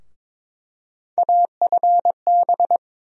Morse code, heard as a keyed tone, about 22 wpm, 700 Hz. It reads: AFB
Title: USA Call Sign Postfixes - 3 Letters 22wpm